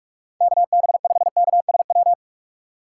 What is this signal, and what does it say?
Morse code, heard as a keyed tone, about 42 wpm, 700 Hz. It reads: KB5KSW
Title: USA Call Signs - 6 Characters - 6x Character Spacing - 45wpm